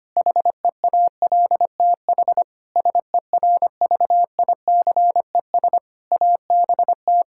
Morse code, heard as a keyed tone, about 25 wpm, 700 Hz. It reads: HEALTH SERVICES ABT
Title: Practice QSO - Exchange 3 - More Info - 25wpm